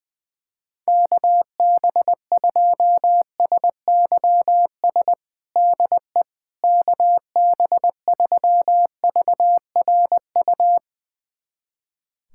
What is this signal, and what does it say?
Morse code, heard as a keyed tone, about 21 wpm, 700 Hz. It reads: KB2SYS DE KB3VRU
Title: USA Callsign DE Callsign - 6 Characters - 20wpm